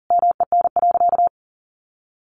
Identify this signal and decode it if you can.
Morse code, heard as a keyed tone, about 40 wpm, 700 Hz. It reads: MEN.
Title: Sentences from Top 1300 Words 40wpm